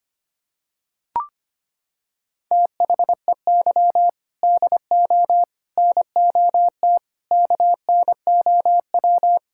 Morse code, heard as a keyed tone, about 25 wpm, 700 Hz. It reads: THEY DO NOT KNOW
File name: practice / 25wpm / Sentences with 4-Word Phrases - 25wpm